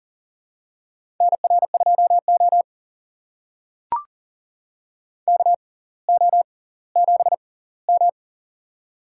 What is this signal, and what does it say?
Morse code, heard as a keyed tone, about 40 wpm, spaced as for 20 wpm, 700 Hz. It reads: NR2O KO7M
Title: USA Call Signs - 4 Characters - 6x Character Spacing - 40wpm